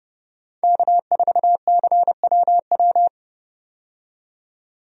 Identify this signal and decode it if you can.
Morse code, heard as a keyed tone, about 30 wpm, 700 Hz. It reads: K4CWW
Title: USA Call Signs - 5 Characters - 30wpm